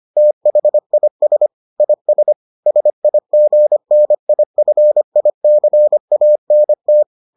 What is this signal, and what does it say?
Morse code, heard as a keyed tone, about 25 wpm, 600 Hz. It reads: THIS IS SIGNIFICANT